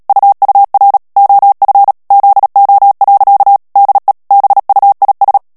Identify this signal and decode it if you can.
Morse code, heard as a keyed tone, about 37 wpm, 800 Hz. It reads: UUR OF ZO. DE BUIS
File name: morse mp3 / HOLLANDAIS texte clair 2 vitesse 35 mots minute